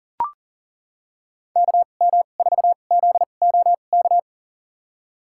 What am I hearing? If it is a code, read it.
Morse code, heard as a keyed tone, about 40 wpm, spaced as for 29 wpm, 700 Hz. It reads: KM4ZOK
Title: USA Call Signs - 6 Characters - 2x Character Spacing - 40wpm